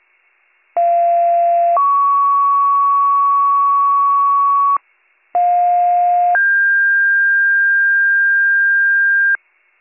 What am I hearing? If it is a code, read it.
Two-tone page, heard as a one-second tone, then a three-second tone, about 690 then 1120 Hz, about 710 then 1600 Hz, with noise. It is heard from 0.8 to 9.4 s.